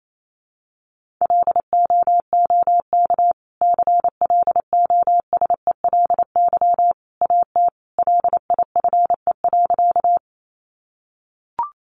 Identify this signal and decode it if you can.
Morse code, heard as a keyed tone, about 28 wpm, 700 Hz. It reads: LOOK CLOSELY AT LIFE.
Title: Sentences from Top 1400 Words 28wpm